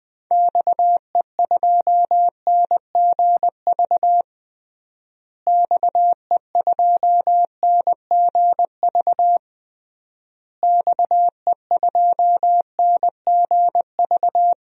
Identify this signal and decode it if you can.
Morse code, heard as a keyed tone, about 21 wpm, 700 Hz. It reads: XE2NGV XE2NGV XE2NGV
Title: DXCC-100 Call Signs - 6 Characters - Repeated 3x - 20wpm